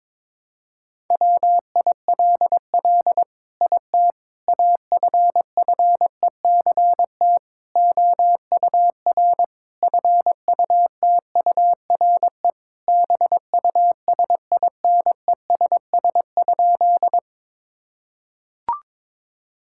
Morse code, heard as a keyed tone, about 22 wpm, 700 Hz. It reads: WILL IT AFFECT OUR FUTURE BUSINESS?
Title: Sentences from Top 1000 Words - Review Part 6 - 22wpm